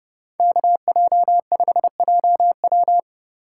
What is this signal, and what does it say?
Morse code, heard as a keyed tone, about 32 wpm, 700 Hz. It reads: KJ5JW